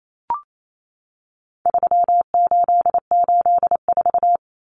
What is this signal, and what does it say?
Morse code, heard as a keyed tone, about 28 wpm, 700 Hz. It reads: 3884
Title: Common Cut Numbers - 4-Digit 28wpm